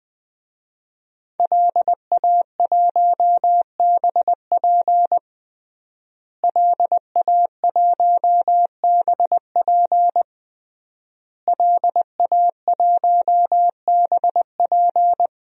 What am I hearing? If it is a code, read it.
Morse code, heard as a keyed tone, about 20 wpm, 700 Hz. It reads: LA1BP LA1BP LA1BP
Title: DXCC-100 Call Signs - 5 Characters - Repeated 3x - 20wpm